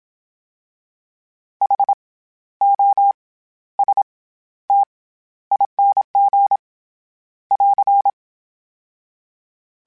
Morse code, heard as a keyed tone, about 27 wpm, 800 Hz. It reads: H O S T ING <AR>
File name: Special INGs-mp3s / Special INGs-mp3s / SlowToFaster ing @25